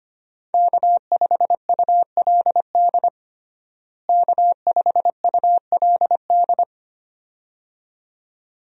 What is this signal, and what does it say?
Morse code, heard as a keyed tone, about 25 wpm, 700 Hz. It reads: K5ULD K5ULD
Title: USA Call Signs - 5 Characters - Repeated 2x - 25wpm